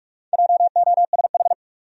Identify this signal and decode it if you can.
Morse code, heard as a keyed tone, about 42 wpm, 700 Hz. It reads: JOSH
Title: USA Common Names - Max 5 Letters - 45wpm